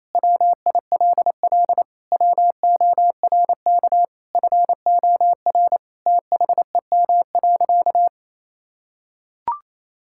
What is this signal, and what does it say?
Morse code, heard as a keyed tone, about 28 wpm, 700 Hz. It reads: WILL WORK FOR THEM.
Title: Sentences from Top 1000 Words - Review Part 5 - 28wpm